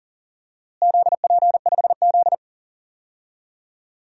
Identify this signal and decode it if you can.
Morse code, heard as a keyed tone, about 42 wpm, 690 Hz. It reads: ZP5Z
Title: DXCC-100 Call Signs - 4 Characters - Repeated 2x - 40wpm